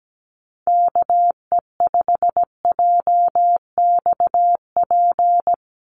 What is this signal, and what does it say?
Morse code, heard as a keyed tone, about 17 wpm, 700 Hz. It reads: KE5JXP
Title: USA Call Signs 17wpm